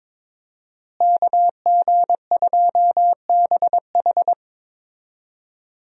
Morse code, heard as a keyed tone, about 22 wpm, 700 Hz. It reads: KG2BH
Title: USA Call Signs - 5 Characters - Repeated 3x - 22wpm